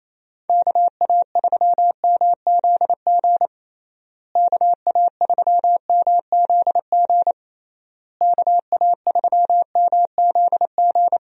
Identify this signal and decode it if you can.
Morse code, heard as a keyed tone, about 28 wpm, 700 Hz. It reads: KA3MZG KA3MZG KA3MZG
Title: USA Call Signs - 6 Characters - Repeated 3x - 28wpm